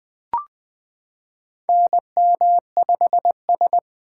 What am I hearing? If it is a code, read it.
Morse code, heard as a keyed tone, about 21 wpm, 700 Hz. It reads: NM5S
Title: 2023 SOTA Activator Call Signs - 20wpm